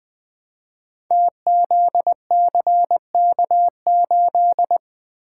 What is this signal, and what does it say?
Morse code, heard as a keyed tone, about 20 wpm, 700 Hz. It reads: TZCK8
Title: Code Groups - 5 Letter:Numbers - 20wpm